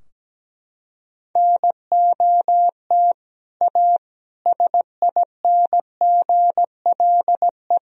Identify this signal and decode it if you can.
Morse code, heard as a keyed tone, about 17 wpm, 700 Hz. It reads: NOT A SINGLE
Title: Sentences from Top 1000 Words - Review Part 4 - 17wpm